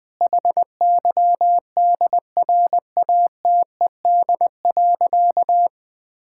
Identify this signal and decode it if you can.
Morse code, heard as a keyed tone, about 20 wpm, 700 Hz. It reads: HYDRATED.
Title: Difficult Sentences with Hints - 20wpm